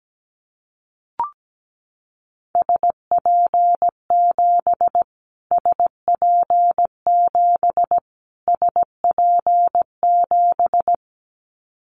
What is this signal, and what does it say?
Morse code, heard as a keyed tone, about 17 wpm, 700 Hz. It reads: SP7 SP7 SP7
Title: Callsign Prefixes for Top 25 DXCC - Repeated 3x - 17wpm